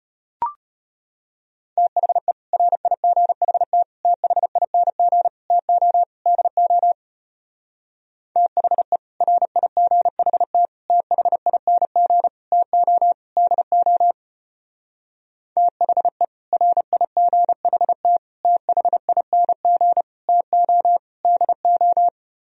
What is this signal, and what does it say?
Morse code, heard as a keyed tone, about 31 wpm, 700 Hz. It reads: THE RIGHT THING TO DO THE RIGHT THING TO DO THE RIGHT THING TO DO